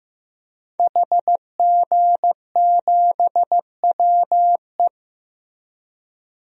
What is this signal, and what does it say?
Morse code, heard as a keyed tone, about 15 wpm, 700 Hz. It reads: HG7WE
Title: DXCC-100 Call Signs - 5 Characters - Repeated 2x - 15wpm